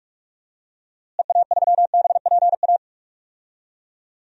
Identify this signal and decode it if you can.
Morse code, heard as a keyed tone, about 41 wpm, 700 Hz. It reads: EA3BPA
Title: DXCC-100 Call Signs - 6 Characters - Repeated 2x - 45wpm